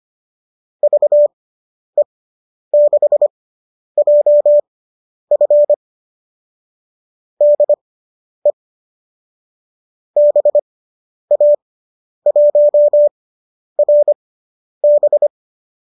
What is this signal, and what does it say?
Morse code, heard as a keyed tone, about 25 wpm, spaced as for 10 wpm, 570 Hz. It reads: VE6JF DE BA1RB